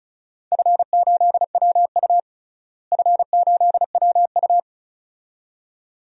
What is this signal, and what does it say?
Morse code, heard as a keyed tone, about 34 wpm, 700 Hz. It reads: F8WU F8WU
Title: DXCC-100 Call Signs - 4 Characters - Repeated 2x - 35wpm